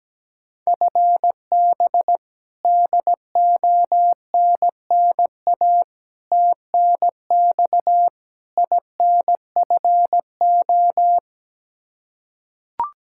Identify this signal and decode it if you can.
Morse code, heard as a keyed tone, about 17 wpm, 700 Hz. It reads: FB DONNA TNX INFO
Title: Practice QSO - Exchange 3 - More Info - 17wpm